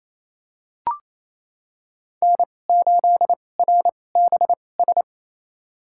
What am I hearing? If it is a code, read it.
Morse code, heard as a keyed tone, about 28 wpm, spaced as for 20 wpm, 700 Hz. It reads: N8RBS
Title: USA Call Signs - 5 Characters - 2x Character Spacing - 28wpm